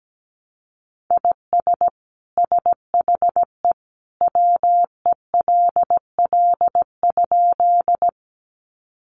Morse code, heard as a keyed tone, about 17 wpm, 700 Hz. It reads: IS SHE WELL?
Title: Sentences from Top 200 Words 17wpm